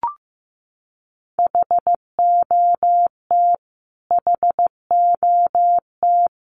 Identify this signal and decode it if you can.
Morse code, heard as a keyed tone, about 15 wpm, 700 Hz. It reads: HOT HOT
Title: Top 500 Words Repeated 15wpm